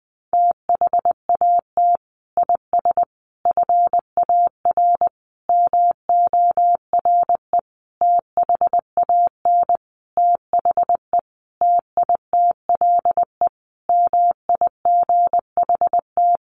Morse code, heard as a keyed tone, about 20 wpm, 700 Hz. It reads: THAT IS FAR MORE THAN THE TITLE MIGHT
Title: Sentences from Top 1200 Words 20wpm